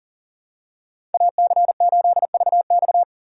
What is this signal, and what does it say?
Morse code, heard as a keyed tone, about 40 wpm, 700 Hz. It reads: AC8VX
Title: USA Call Signs - 5 Characters - 2x Character Spacing - 40wpm